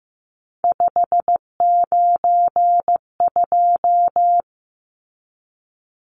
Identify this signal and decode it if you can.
Morse code, heard as a keyed tone, about 15 wpm, 700 Hz. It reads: 592
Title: Numbers - 3-Digit 15wpm